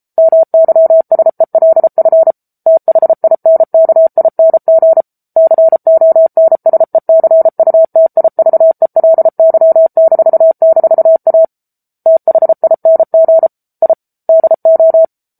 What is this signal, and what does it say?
Morse code, heard as a keyed tone, about 34 wpm, 650 Hz. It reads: MYSELF THINKING CONSECUTIVELY--A THING I DO